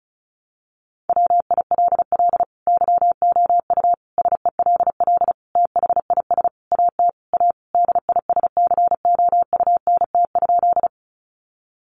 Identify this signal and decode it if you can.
Morse code, heard as a keyed tone, about 34 wpm, 700 Hz. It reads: WILL YOU SELL THIS AT A DISCOUNT?
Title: Sentences from Top 1200 Words 35wpm